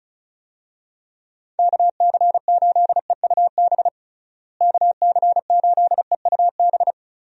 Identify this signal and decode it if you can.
Morse code, heard as a keyed tone, about 36 wpm, 700 Hz. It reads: KC8EUB KC8EUB
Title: USA Call Signs - 6 Characters - Repeated 2x - 35wpm